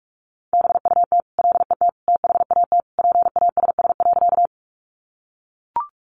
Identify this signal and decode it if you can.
Morse code, heard as a keyed tone, about 44 wpm, 700 Hz. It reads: BUT LET THAT PASS.
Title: Sentences from Top 400 Words 45wpm